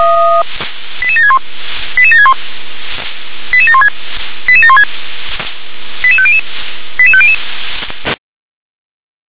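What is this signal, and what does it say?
Five-tone selective call (ZVEI), heard as quick runs of five tones, tones of about 70 ms. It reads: 88761 88761 88716 88716 88500 88500
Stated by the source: ZVEI five-tone call 88788 received by SDR.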